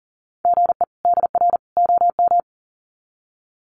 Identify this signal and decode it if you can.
Morse code, heard as a keyed tone, about 40 wpm, 700 Hz. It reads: GE DR OM
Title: Common QSO Phrases 40wpm